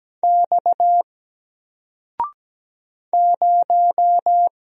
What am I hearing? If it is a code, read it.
Morse code, heard as a keyed tone, about 17 wpm, 700 Hz. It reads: X 0
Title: Single Letter-Number 17wpm